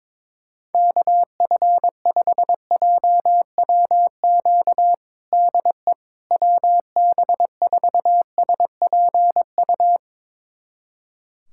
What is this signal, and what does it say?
Morse code, heard as a keyed tone, about 22 wpm, 700 Hz. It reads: KF5JWQ DE WB4SPU
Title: USA Callsign DE Callsign - 6 Characters - 22wpm